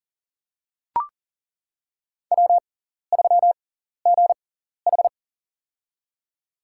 Morse code, heard as a keyed tone, about 40 wpm, spaced as for 14 wpm, 700 Hz. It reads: W3GH
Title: USA Call Signs - 4 Characters - 6x Character Spacing - 40wpm